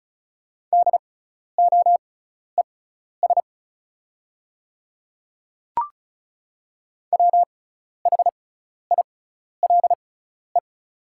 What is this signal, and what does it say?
Morse code, heard as a keyed tone, about 36 wpm, spaced as for 12 wpm, 700 Hz. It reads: DOES WHILE
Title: 6x Character Spacing - Top 500 Words - 35wpm